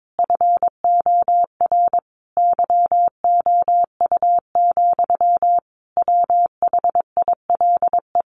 Morse code, heard as a keyed tone, about 22 wpm, 700 Hz. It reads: FOR YOU, WHILE
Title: Sentences from Top 1400 Words 22wpm